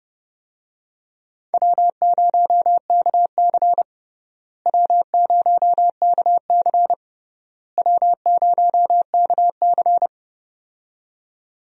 Morse code, heard as a keyed tone, about 30 wpm, 700 Hz. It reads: W0KC W0KC W0KC
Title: USA Call Signs - 4 Characters - Repeated 3x - 30wpm